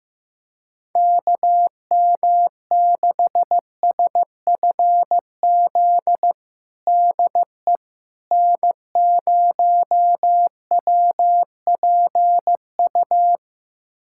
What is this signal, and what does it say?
Morse code, heard as a keyed tone, about 16 wpm, 700 Hz. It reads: KM6SFZ DE N0WPU
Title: USA Callsign DE Callsign 15wpm